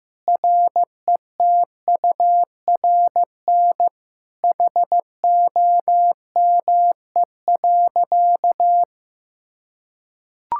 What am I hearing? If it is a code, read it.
Morse code, heard as a keyed tone, about 15 wpm, 700 Hz. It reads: RETURN HOME.
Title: Sentences from Top 1000 Words - Review Part 5 - 15wpm